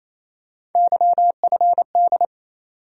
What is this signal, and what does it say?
Morse code, heard as a keyed tone, about 27 wpm, 700 Hz. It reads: YFD
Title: USA Call Sign Postfixes - 3 Letters 28wpm